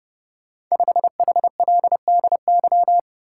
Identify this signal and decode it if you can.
Morse code, heard as a keyed tone, about 28 wpm, 700 Hz. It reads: 5HLDY
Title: Code Groups - 5 Letter:Numbers - 30wpm